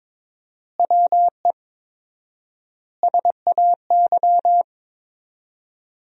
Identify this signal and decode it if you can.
Morse code, heard as a keyed tone, about 22 wpm, 700 Hz. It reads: WE SAY
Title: 4x Word-Spacing - Sentences from Top 200 Words - 22wpm